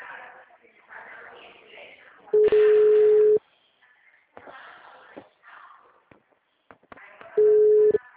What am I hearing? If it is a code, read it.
Ringback tone (Europe, 425 Hz), with speech.
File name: out- -2045- - - .11011